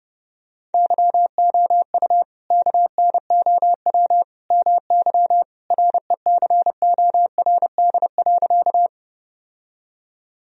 Morse code, heard as a keyed tone, about 30 wpm, 700 Hz. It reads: YOU KNOW MY RECORD.
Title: Sentences from Top 1000 Words - Review Part 3 - 30wpm